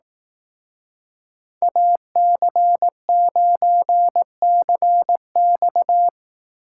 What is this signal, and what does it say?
Morse code, heard as a keyed tone, about 18 wpm, 700 Hz. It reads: AC9CX
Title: Speed Racing - USA 2x2 Call Signs 15wpm